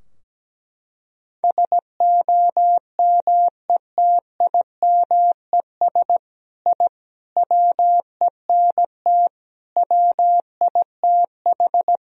Morse code, heard as a keyed tone, about 17 wpm, 700 Hz. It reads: SOMETIMES I WENT WITH
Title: Sentences from Top 1000 Words - Review Part 3 - 17wpm